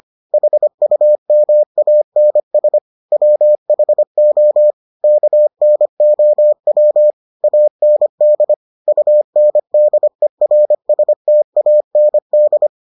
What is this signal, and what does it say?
Morse code, heard as a keyed tone, about 25 wpm, 600 Hz. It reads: HUMANS WHO KNOW AND UNDERSTAND